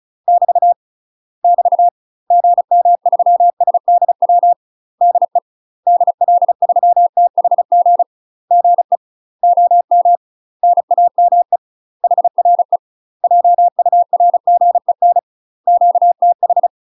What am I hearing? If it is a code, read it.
Morse code, heard as a keyed tone, about 34 wpm, 700 Hz. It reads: <BT> <BT> GM3SDW DE DL3THG GE OM NAME HRE JURGEN QTH
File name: UK MASTER QSOs TRAINING / 35WPM / UK MASTER QSOs 0015